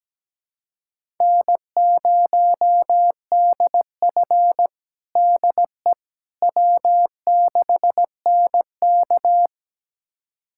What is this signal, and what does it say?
Morse code, heard as a keyed tone, about 17 wpm, 700 Hz. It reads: N0DF DE W6NK
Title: USA Callsign DE Callsign - 4 Characters - 17wpm